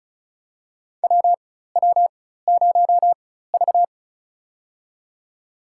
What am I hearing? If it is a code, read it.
Morse code, heard as a keyed tone, about 34 wpm, spaced as for 16 wpm, 700 Hz. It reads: WW0V
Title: USA Call Signs - 4 Characters - 4x Character Spacing - 35wpm